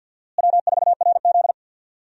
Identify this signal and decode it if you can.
Morse code, heard as a keyed tone, about 50 wpm, 700 Hz. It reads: W4RZ
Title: USA Call Signs - 4 Characters - Repeated 3x - 50wpm